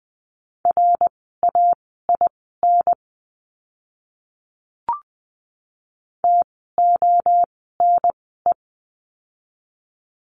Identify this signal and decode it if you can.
Morse code, heard as a keyed tone, about 20 wpm, spaced as for 14 wpm, 700 Hz. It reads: RAIN TONE
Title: 068 Lesson 12f - Letter d - Words 20wpm